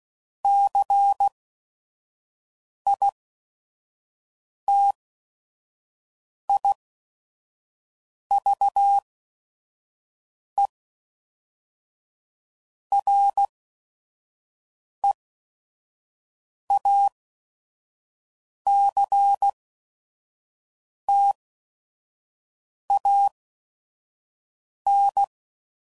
Morse code, heard as a keyed tone, about 16 wpm, spaced as for 5 wpm, 800 Hz. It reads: CITIVE REACTAN
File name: CW Code Groups @5WPM / Technical Subjects / XC